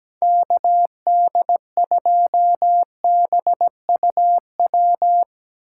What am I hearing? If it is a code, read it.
Morse code, heard as a keyed tone, about 17 wpm, 700 Hz. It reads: KD2BUW
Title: USA Call Signs - 2x Character Spacing - 17wpm